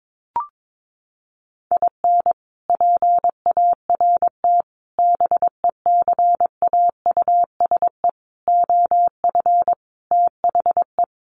Morse code, heard as a keyed tone, about 22 wpm, 700 Hz. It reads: IN PART BECAUSE OF THE
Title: Sets of 5 Words Encore2 22wpm